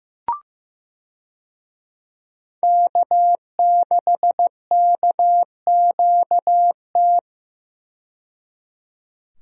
Morse code, heard as a keyed tone, about 16 wpm, 700 Hz. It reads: K6KQT
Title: USA Call Signs - 5 Characters - 15wpm